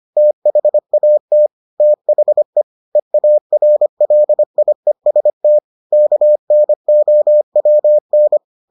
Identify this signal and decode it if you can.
Morse code, heard as a keyed tone, about 25 wpm, 600 Hz. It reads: THAT THE EARLIEST KNOWN